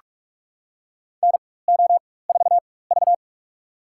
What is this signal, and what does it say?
Morse code, heard as a keyed tone, about 46 wpm, spaced as for 21 wpm, 700 Hz. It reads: NO4V
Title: USA Call Signs - 4 Characters - 4x Character Spacing - 45wpm